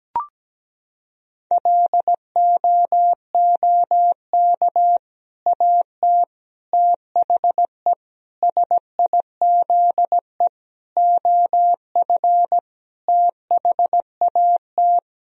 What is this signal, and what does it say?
Morse code, heard as a keyed tone, about 17 wpm, 700 Hz. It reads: LOOK AT THE SIZE OF THAT
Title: Sentences from Top 1000 Words - Review Part 3 - 17wpm